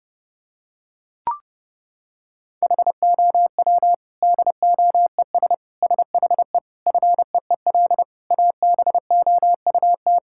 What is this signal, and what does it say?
Morse code, heard as a keyed tone, about 30 wpm, 700 Hz. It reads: HOW DOES SHE FEEL ABOUT
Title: Sentences from Top 1000 Words - Review Part 6 - 30wpm